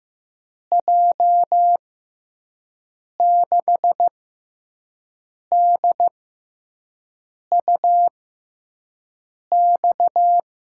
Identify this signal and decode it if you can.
Morse code, heard as a keyed tone, about 15 wpm, spaced as for 5 wpm, 700 Hz. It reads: J6DUX
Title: USA Call Signs - 6x Character Spacing - 15wpm